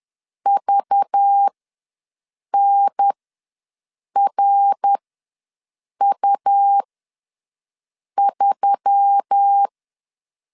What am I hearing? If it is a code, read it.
Morse code, heard as a keyed tone, about 10 wpm, spaced as for 6 wpm, 800 Hz. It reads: VNRU3
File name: LEZIONI / 5 LBVC / 5 LBV ws 10wpm 3645